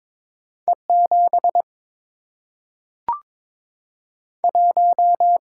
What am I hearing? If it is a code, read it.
Morse code, heard as a keyed tone, about 22 wpm, 700 Hz. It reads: E 7 1
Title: Single-Letter Single-Number 22wpm